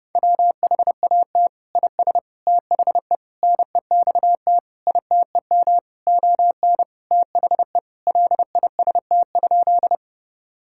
Morse code, heard as a keyed tone, about 30 wpm, 700 Hz. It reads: WHAT IS THE NEXT ITEM ON THE LIST?
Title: Sentences from Top 1000 Words - Review Part 6 - 30wpm